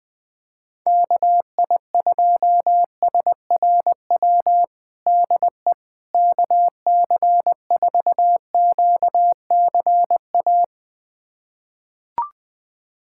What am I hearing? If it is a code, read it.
Morse code, heard as a keyed tone, about 21 wpm, 700 Hz. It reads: KI2SRW DE KC4QCA
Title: USA Callsign DE Callsign 20wpm